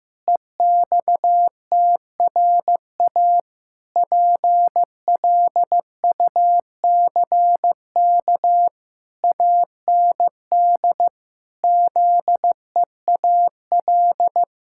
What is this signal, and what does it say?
Morse code, heard as a keyed tone, about 15 wpm, 700 Hz. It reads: EXTRA PLUCK AND ZEAL
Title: Pangram Challenge v2 - 15wpm